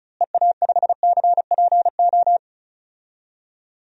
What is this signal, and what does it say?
Morse code, heard as a keyed tone, about 36 wpm, 700 Hz. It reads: EA5CPO
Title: DXCC-100 Call Signs - 6 Characters - Repeated 2x - 35wpm